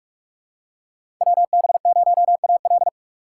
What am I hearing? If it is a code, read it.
Morse code, heard as a keyed tone, about 44 wpm, 700 Hz. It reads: WB0AL
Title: USA Call Signs - 5 Characters - 3x Character Spacing - 45wpm